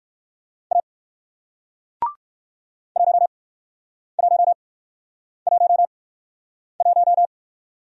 Morse code, heard as a keyed tone, about 59 wpm, spaced as for 9 wpm, 700 Hz. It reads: A 1111